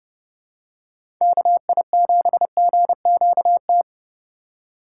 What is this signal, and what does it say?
Morse code, heard as a keyed tone, about 30 wpm, 700 Hz. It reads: KI7GQT
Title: USA Call Signs - 2x Character Spacing - 30wpm